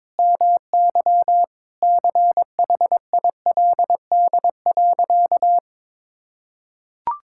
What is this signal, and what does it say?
Morse code, heard as a keyed tone, about 22 wpm, 700 Hz. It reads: MY CHILD.
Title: Sentences from Top 1000 Words - Review Part 6 - 22wpm